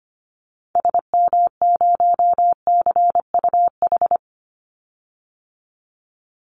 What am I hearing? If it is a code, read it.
Morse code, heard as a keyed tone, about 25 wpm, 700 Hz. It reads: SM0CUH
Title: Call Signs Repeated 3x - 25wpm